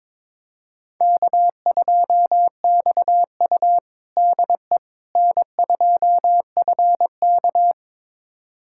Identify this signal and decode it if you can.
Morse code, heard as a keyed tone, about 22 wpm, 700 Hz. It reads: K2XU DE N2FK
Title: USA Callsign DE Callsign - 4 Characters - 22wpm